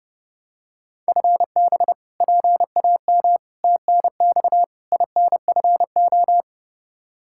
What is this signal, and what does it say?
Morse code, heard as a keyed tone, about 30 wpm, 700 Hz. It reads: FB PAM TNX INFO
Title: K1USN SST QSO Practice 30wpm